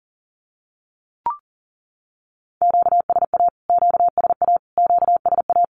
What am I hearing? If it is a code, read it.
Morse code, heard as a keyed tone, about 40 wpm, 700 Hz. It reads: QSA QSA QSA
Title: Q-Codes Repeated 3x 40wpm